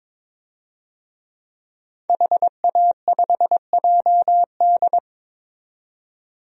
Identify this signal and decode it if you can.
Morse code, heard as a keyed tone, about 22 wpm, 700 Hz. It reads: HA5JD
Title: DXCC-100 Call Signs - 5 Characters - 22wpm